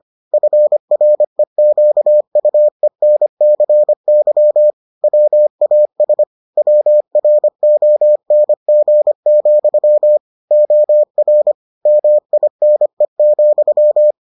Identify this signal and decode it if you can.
Morse code, heard as a keyed tone, about 25 wpm, 600 Hz. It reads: FREQUENCY WAS WRONG, OR MINE,